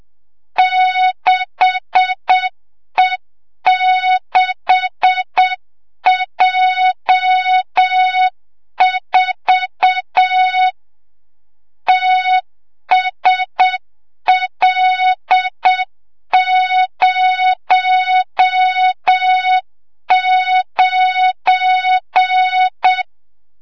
Morse code, heard as a keyed tone, about 7 wpm, 3700 Hz, with noise. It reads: 6E6J4 TSL09
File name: Morze-HA2MN / hang / 25 GYAKORLAT